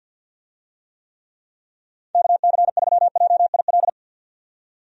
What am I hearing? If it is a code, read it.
Morse code, heard as a keyed tone, about 49 wpm, 700 Hz. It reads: KC3JIL
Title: USA Call Signs - 6 Characters - 50wpm